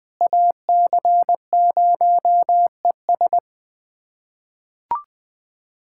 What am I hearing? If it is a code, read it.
Morse code, heard as a keyed tone, about 20 wpm, 700 Hz. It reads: AC0ES
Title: USA Callsign DE Callsign 20wpm